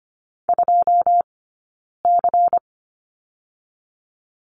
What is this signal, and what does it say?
Morse code, heard as a keyed tone, about 25 wpm, 700 Hz. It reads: 2 C
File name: course / 25:10wpm / 092 Lesson 16d - Letter c - Two Characters 25wpm